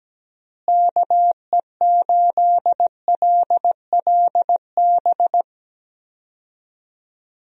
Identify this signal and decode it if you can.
Morse code, heard as a keyed tone, about 17 wpm, 700 Hz. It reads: KE8LLB
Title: USA Call Signs - 6 Characters - Repeated 2x - 17wpm